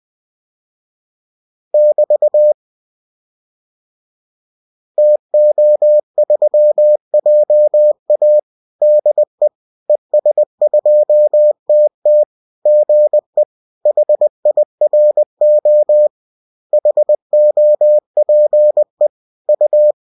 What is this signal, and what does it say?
Morse code, heard as a keyed tone, about 20 wpm, 600 Hz. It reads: <BT> TO3JA DE ES2TT GE HIRO HOPE U